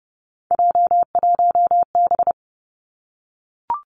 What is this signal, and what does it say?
Morse code, heard as a keyed tone, about 32 wpm, 700 Hz. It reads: J1B
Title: Three Challenging Letter-Number Combinations 30wpm